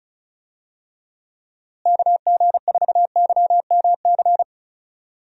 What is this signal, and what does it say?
Morse code, heard as a keyed tone, about 34 wpm, 700 Hz. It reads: KG4YMC
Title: USA Call Signs - 6 Characters - Repeated 2x - 35wpm